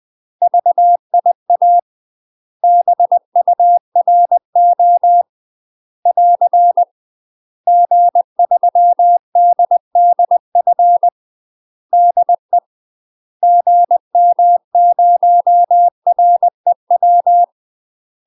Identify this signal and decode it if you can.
Morse code, heard as a keyed tone, about 20 wpm, 700 Hz. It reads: VIA BURO <AR> G3DDF DE GM0REW
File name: UK MASTER QSOs TRAINING / 20WPM / UK MASTER QSOs 0073